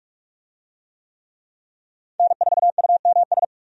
Morse code, heard as a keyed tone, about 46 wpm, 700 Hz. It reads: N4UMS
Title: USA Call Signs - 5 Characters - 45wpm